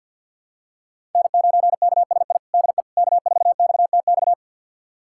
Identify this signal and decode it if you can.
Morse code, heard as a keyed tone, about 48 wpm, 700 Hz. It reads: N9KSI DE K4XTX
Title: USA Callsign DE Callsign - 5 Characters - 50wpm